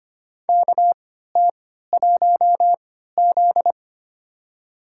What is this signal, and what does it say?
Morse code, heard as a keyed tone, about 25 wpm, spaced as for 14 wpm, 700 Hz. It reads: KT1Z
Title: USA Call Signs - 4 Characters - 3x Character Spacing - 25wpm